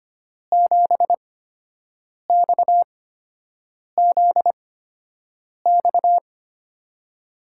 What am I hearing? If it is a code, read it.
Morse code, heard as a keyed tone, about 25 wpm, spaced as for 7 wpm, 700 Hz. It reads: 7XZX
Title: USA Call Signs - 5 Characters - 8x Character Spacing - 25wpm